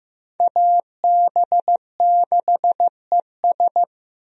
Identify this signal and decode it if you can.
Morse code, heard as a keyed tone, about 15 wpm, 700 Hz. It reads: AB6ES
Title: USA Call Signs - 5 Characters - Repeated 2x - 15wpm